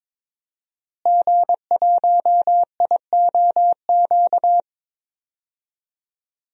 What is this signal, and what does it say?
Morse code, heard as a keyed tone, about 22 wpm, 700 Hz. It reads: G1IOQ
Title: DXCC-100 Call Signs - 5 Characters - Repeated 3x - 22wpm